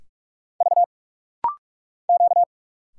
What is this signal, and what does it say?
Morse code, heard as a keyed tone, about 46 wpm, 700 Hz. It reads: V Q
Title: Single Letters - Rapid-Fire 45wpm